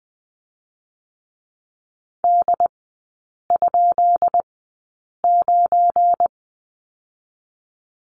Morse code, heard as a keyed tone, about 20 wpm, spaced as for 8 wpm, 690 Hz. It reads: D?9
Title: 212 Lesson 34e - Number 8 - Three Characters 20wpm